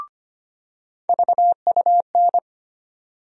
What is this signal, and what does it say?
Morse code, heard as a keyed tone, about 25 wpm, 700 Hz. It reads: VUN
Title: USA Call Sign Postfixes - 3 Letters 25wpm